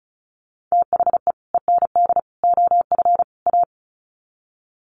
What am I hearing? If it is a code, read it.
Morse code, heard as a keyed tone, about 34 wpm, 700 Hz. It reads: THE END OF A
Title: Sets of 4 Words 35wpm